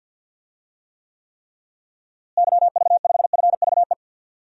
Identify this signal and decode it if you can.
Morse code, heard as a keyed tone, about 48 wpm, 700 Hz. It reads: YV5FVE
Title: DXCC-100 Call Signs - 6 Characters - Repeated 3x - 50wpm